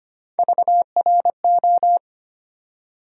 Morse code, heard as a keyed tone, about 25 wpm, 700 Hz. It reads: VRO
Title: USA Call Sign Postfixes - 3 Letters 25wpm